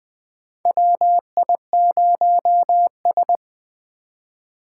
Morse code, heard as a keyed tone, about 20 wpm, 700 Hz. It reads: WI0S
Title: 2023 POTA Activator Call Signs - 20wpm